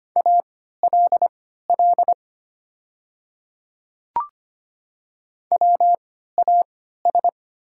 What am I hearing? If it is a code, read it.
Morse code, heard as a keyed tone, about 25 wpm, spaced as for 14 wpm, 700 Hz. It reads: ALL WAS
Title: 3x Character Spacing - Top 200 Words - 25wpm